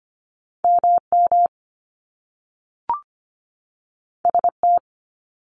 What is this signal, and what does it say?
Morse code, heard as a keyed tone, about 25 wpm, 700 Hz. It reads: MM ST
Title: Top Two Letter Combinations in Words 25wpm